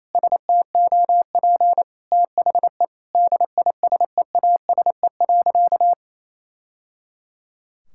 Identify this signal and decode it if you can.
Morse code, heard as a keyed tone, about 27 wpm, 700 Hz. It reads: STOP THE DISEASE.
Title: Sentences from Top 1000 Words - Review Part 6 - 28wpm